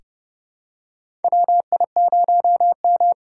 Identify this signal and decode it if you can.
Morse code, heard as a keyed tone, about 32 wpm, 700 Hz. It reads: WI0M
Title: USA Call Signs - 8x Character Spacing - 30wpm